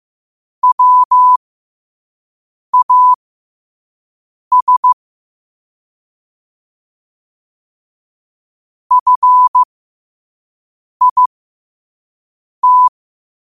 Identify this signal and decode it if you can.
Morse code, heard as a keyed tone, about 15 wpm, spaced as for 5 wpm, 1000 Hz. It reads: WAS FIT
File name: qotdfiles / 5wpm